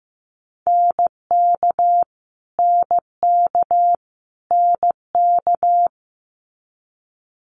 Morse code, heard as a keyed tone, about 15 wpm, 700 Hz. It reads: NK NK NK
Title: Multi-letter Phonemes Repeated 3x 15wpm